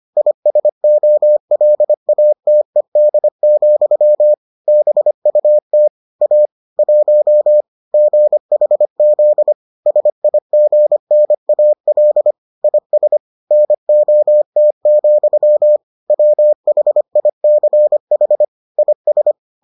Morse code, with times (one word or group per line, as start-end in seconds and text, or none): 0.2-4.3 ISOLATED,
4.7-5.9 BUT
6.2-6.5 A
6.8-7.6 1
7.9-9.5 GHZ
9.9-12.3 SIGNAL
12.6-13.2 IS
13.5-15.8 NOT,
16.1-18.4 WHICH
18.8-19.3 IS